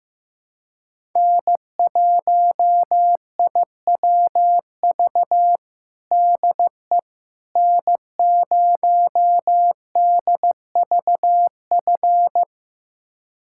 Morse code, heard as a keyed tone, about 15 wpm, 700 Hz. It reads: N1IWV DE N0DVF
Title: USA Callsign DE Callsign - 5 Characters - 15wpm